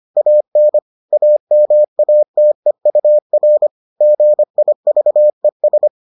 Morse code, heard as a keyed tone, about 25 wpm, 600 Hz. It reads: AN AMATEUR GIVES